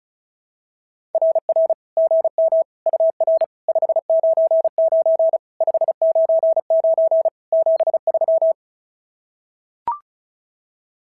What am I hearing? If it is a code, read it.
Morse code, heard as a keyed tone, about 32 wpm, 650 Hz. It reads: RR GM UR 599 599 73
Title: SOTA QSO Practice 35wpm